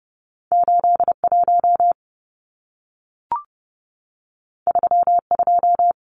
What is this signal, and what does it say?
Morse code, heard as a keyed tone, about 30 wpm, 700 Hz. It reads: 81 32
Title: Numbers - 2-Digit 30wpm